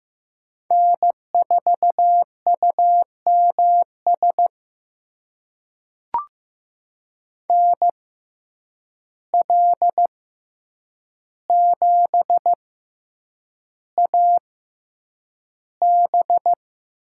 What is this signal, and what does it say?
Morse code, heard as a keyed tone, about 15 wpm, spaced as for 8 wpm, 700 Hz. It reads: N4UMS NL7AB
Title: USA Call Signs - 5 Characters - 6x Character Spacing - 15wpm